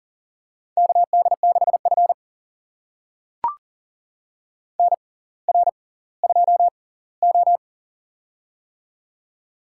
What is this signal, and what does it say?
Morse code, heard as a keyed tone, about 40 wpm, spaced as for 20 wpm, 700 Hz. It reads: KD6F NR2O
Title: USA Call Signs - 4 Characters - 6x Character Spacing - 40wpm